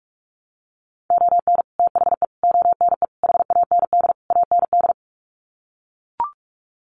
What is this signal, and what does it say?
Morse code, heard as a keyed tone, about 44 wpm, 700 Hz. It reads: ON THE ONE HAND AND
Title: Sets of 5 Words Encore2 45wpm